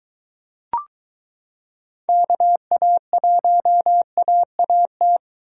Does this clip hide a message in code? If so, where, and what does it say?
Yes: Morse code from 2.1 to 5.2 s, heard as a keyed tone, about 23 wpm, 700 Hz. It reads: KA1AAT